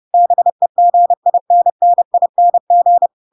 Morse code, heard as a keyed tone, about 30 wpm, 700 Hz. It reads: BEGINNING
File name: Pinocchio / Pinocchio ch4 30 30 ik1hge